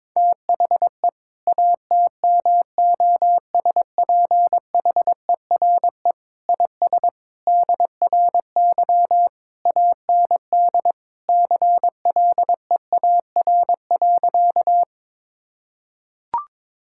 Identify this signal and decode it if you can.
Morse code, heard as a keyed tone, about 22 wpm, 700 Hz. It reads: THE ATMOSPHERE IS DRY AND CLEAR.
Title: Sentences from Top 1600 Words 22wpm